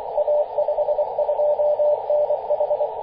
Morse code, heard as a keyed tone, about 24 wpm, 590 Hz, with radio noise. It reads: A52NH
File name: posts / 2016-mar-20-0547z-a52nh-24-905mhz-cw